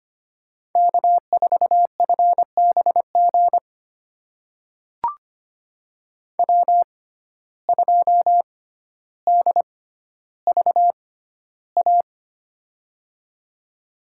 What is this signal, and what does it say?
Morse code, heard as a keyed tone, about 25 wpm, spaced as for 13 wpm, 700 Hz. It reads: K4FBG W2DVA
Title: USA Call Signs - 5 Characters - 6x Character Spacing - 25wpm